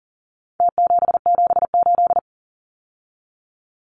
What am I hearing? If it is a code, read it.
Morse code, heard as a keyed tone, about 40 wpm, 700 Hz. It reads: T778
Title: Numbers - Contest Style - 4-Digit 40wpm